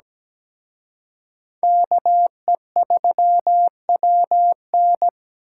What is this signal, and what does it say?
Morse code, heard as a keyed tone, about 20 wpm, 700 Hz. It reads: KE3WN
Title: Speed Racing - USA 2x2 Call Signs 17wpm